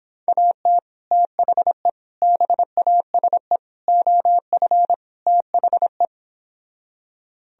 Morse code, heard as a keyed tone, about 26 wpm, 700 Hz. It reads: AT THE BASE OF THE